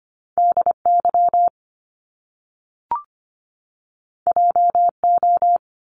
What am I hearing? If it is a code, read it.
Morse code, heard as a keyed tone, about 25 wpm, 700 Hz. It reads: DY JO
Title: Top Two Letter Combinations in Words 25wpm